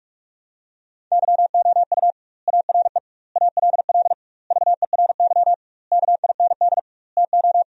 Morse code, heard as a keyed tone, about 46 wpm, 700 Hz. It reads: YOU ARE ALL VERY KIND TO
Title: 2x Word-Spacing - Sentences from Top 200 Words - 45wpm